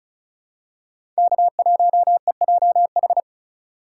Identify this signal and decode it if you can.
Morse code, heard as a keyed tone, about 36 wpm, 700 Hz. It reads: K1EJH
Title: USA Call Signs - 4x Character Spacing - 35wpm